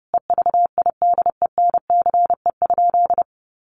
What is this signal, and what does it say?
Morse code, heard as a keyed tone, about 30 wpm, 700 Hz. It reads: EVIDENCE?
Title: Sentences from Top 1300 Words 30wpm